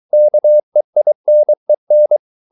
Morse code, heard as a keyed tone, about 23 wpm, 600 Hz. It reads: KEINEN